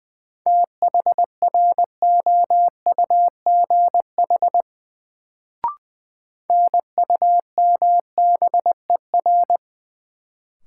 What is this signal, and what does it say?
Morse code, heard as a keyed tone, about 21 wpm, 700 Hz. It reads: THROUGH NUMBER
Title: Sets of 2 Words - New Word Drill - 20wpm